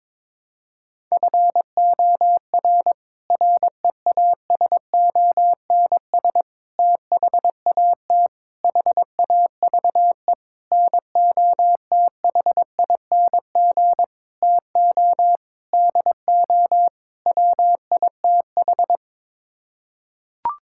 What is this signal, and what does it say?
Morse code, heard as a keyed tone, about 22 wpm, 700 Hz. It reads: FOR REASONS THAT HAVE NOTHING TO DO WITH
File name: practice / 22wpm / Sets of 8 Words - 22wpm